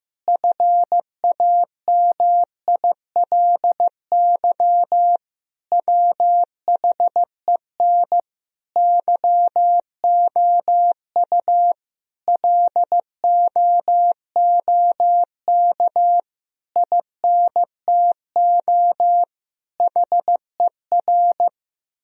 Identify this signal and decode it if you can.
Morse code, heard as a keyed tone, about 16 wpm, 700 Hz. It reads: FAMILY WHEN YOU LOOK INTO HER
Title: Sentences from Top 1000 Words - Review Part 4 - 15wpm